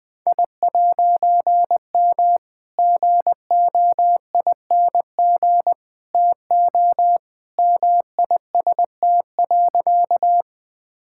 Morse code, heard as a keyed tone, about 20 wpm, 700 Hz. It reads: I'M GOING TO MIST.
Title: All-Age Appropriate Jokes - 20wpm